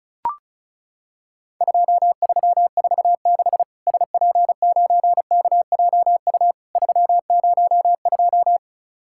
Morse code, heard as a keyed tone, about 36 wpm, 700 Hz. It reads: 2346 SP9KJU 302
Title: Worked All Europe DX Contest - QTC Exchange - 35wpm